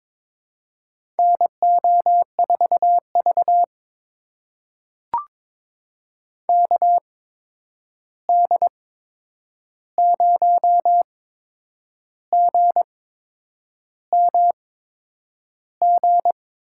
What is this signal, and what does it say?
Morse code, heard as a keyed tone, about 22 wpm, spaced as for 8 wpm, 700 Hz. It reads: NO4V KD0GMG
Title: USA Call Signs - 8x Character Spacing - 22wpm